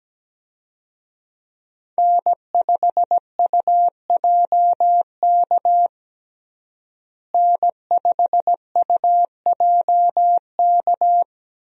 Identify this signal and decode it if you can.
Morse code, heard as a keyed tone, about 17 wpm, 700 Hz. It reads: N5UJK N5UJK
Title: USA Call Signs - 5 Characters - Repeated 3x - 17wpm